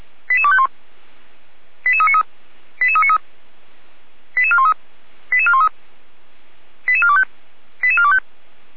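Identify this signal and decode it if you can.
Five-tone selective call (ZVEI), heard as quick runs of five tones, tones of about 70 ms. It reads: 88251 88393 88393 88413 88413 88425 88425
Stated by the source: ZVEI five-tone selective call received on an SDR.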